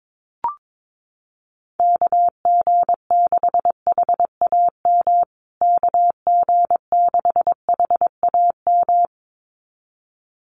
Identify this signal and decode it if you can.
Morse code, heard as a keyed tone, about 22 wpm, 700 Hz. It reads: KG6HAM KG6HAM
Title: USA Call Signs Repeated 2x 22wpm